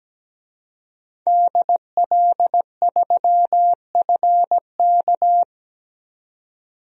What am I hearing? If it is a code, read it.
Morse code, heard as a keyed tone, about 17 wpm, 700 Hz. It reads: DL3FK
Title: DXCC-100 Call Signs - 5 Characters - 17wpm